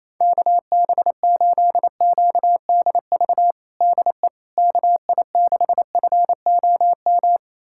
Morse code, heard as a keyed tone, about 27 wpm, 700 Hz. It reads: KB8QDV DE KI6FOM
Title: USA Callsign DE Callsign - 6 Characters - 28wpm